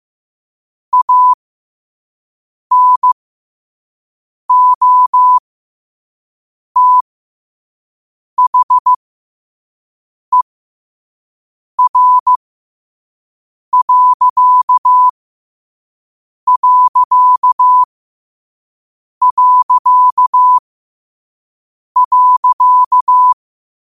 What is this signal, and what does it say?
Morse code, heard as a keyed tone, about 15 wpm, spaced as for 5 wpm, 1000 Hz. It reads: ANOTHER....
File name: qotdfiles / 5wpm